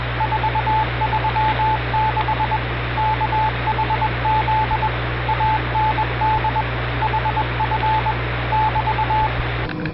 Morse code, heard as a keyed tone, about 21 wpm, 870 Hz, with radio noise. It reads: 436 KHZ AND HF <BT>